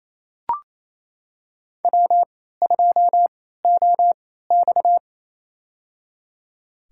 Morse code, heard as a keyed tone, about 28 wpm, spaced as for 16 wpm, 700 Hz. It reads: W2OX
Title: USA Call Signs - 3x Character Spacing - 28wpm